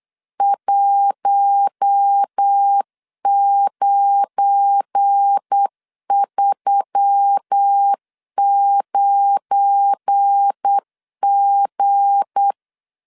Morse code, heard as a keyed tone, about 8 wpm, 800 Hz. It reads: 1939G
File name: LEZIONI / 7 GF9Z / 7 GF9 8wpm 3545